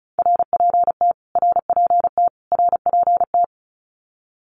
Morse code, heard as a keyed tone, about 34 wpm, 700 Hz. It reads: RPT RPT RPT
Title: Abbreviations Repeated 3x - 35wpm